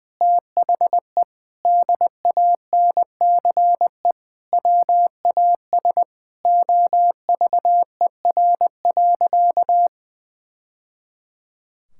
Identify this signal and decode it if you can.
Morse code, heard as a keyed tone, about 20 wpm, 700 Hz. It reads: THE DANCE WAS OVER.
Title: Sentences from Top 1000 Words - Review Part 6 - 20wpm